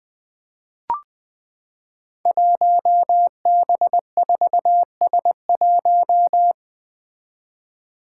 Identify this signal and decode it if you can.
Morse code, heard as a keyed tone, about 20 wpm, 700 Hz. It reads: 1B4S1
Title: Five Challenging Letter-Number Combinations 20wpm